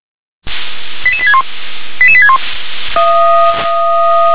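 Five-tone selective call (ZVEI), heard as quick runs of five tones, tones of about 70 ms. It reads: 88861 88861
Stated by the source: Received on SDR.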